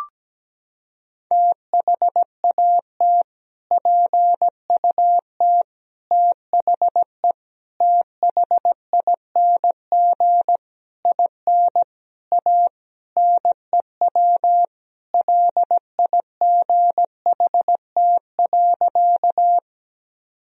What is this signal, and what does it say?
Morse code, heard as a keyed tone, about 17 wpm, 700 Hz. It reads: THAT PUT THE THING IN A NEW LIGHT.
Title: Sentences from Top 1000 Words - Review Part 1 - 17wpm